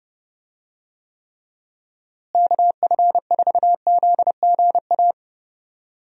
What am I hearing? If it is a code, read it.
Morse code, heard as a keyed tone, about 32 wpm, 700 Hz. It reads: KF4ZGA
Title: USA Call Signs - 6 Characters - Repeated 3x - 30wpm